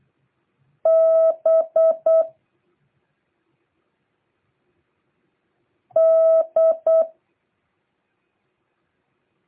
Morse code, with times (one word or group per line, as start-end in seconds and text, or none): 0.9-2.2 B
6.0-7.0 D